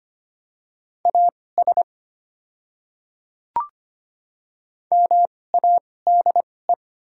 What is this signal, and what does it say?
Morse code, heard as a keyed tone, about 25 wpm, spaced as for 18 wpm, 690 Hz. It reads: AS MADE
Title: 2x Character Spacing - Top 200 Words - 25wpm